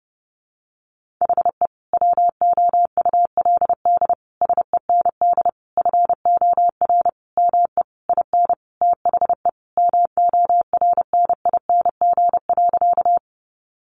Morse code, heard as a keyed tone, about 30 wpm, 700 Hz. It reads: HE WOULD SEND FOR ME IN THE MORNING.
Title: Sentences from Top 1500 Words 30wpm